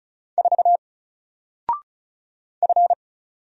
Morse code, heard as a keyed tone, about 36 wpm, 700 Hz. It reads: 4 F
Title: Single Letter-Number Alt 35wpm